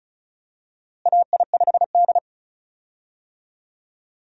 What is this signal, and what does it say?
Morse code, heard as a keyed tone, about 36 wpm, 700 Hz. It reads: AI5D
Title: USA Call Signs - 4 Characters - Repeated 3x - 35wpm